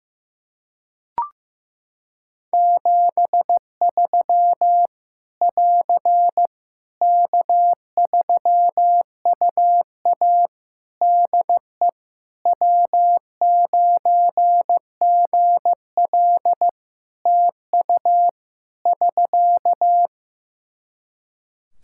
Morse code, heard as a keyed tone, about 15 wpm, 700 Hz. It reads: 73 <AR> K3UA DE W9GL TU <SK>
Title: Practice QSO - Exchange 4 - Ending - 15wpm